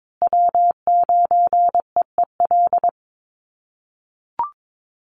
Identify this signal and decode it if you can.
Morse code, heard as a keyed tone, about 22 wpm, 700 Hz. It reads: W9EEL
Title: USA Call Signs Repeated 2x 22wpm